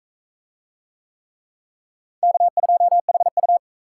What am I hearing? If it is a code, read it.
Morse code, heard as a keyed tone, about 37 wpm, 700 Hz. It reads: K2HU